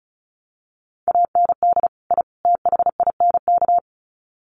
Morse code, heard as a keyed tone, about 34 wpm, 700 Hz. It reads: AND I THINK
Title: Sets of 3 Words 35wpm